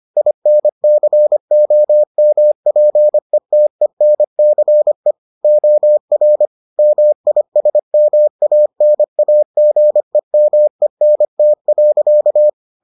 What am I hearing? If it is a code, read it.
Morse code, heard as a keyed tone, about 25 wpm, 600 Hz. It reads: INCOMPETENCE OR MISMANAGEMENT.